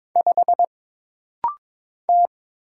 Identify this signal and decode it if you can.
Morse code, heard as a keyed tone, about 22 wpm, 700 Hz. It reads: 5 T
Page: This rapid-fire timing is geared to push your ICR (Instant Character Recognition) skills.